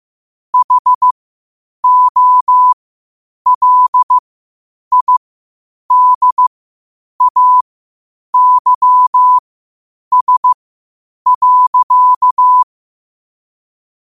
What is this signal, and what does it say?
Morse code, heard as a keyed tone, about 15 wpm, spaced as for 8 wpm, 1000 Hz. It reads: HOLIDAYS.
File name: qotdfiles / 8wpm